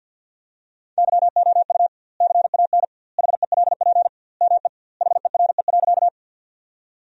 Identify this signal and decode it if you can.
Morse code, heard as a keyed tone, about 47 wpm, 700 Hz. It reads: YOU CAN HELP ME HERE.
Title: 2x Word-Spacing - Sentences from Top 200 Words - 50wpm